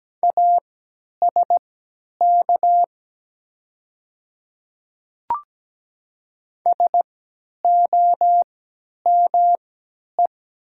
Morse code, heard as a keyed tone, about 17 wpm, spaced as for 10 wpm, 700 Hz. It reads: ASK SOME
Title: 3x Character Spacing - Top 200 Words - 17wpm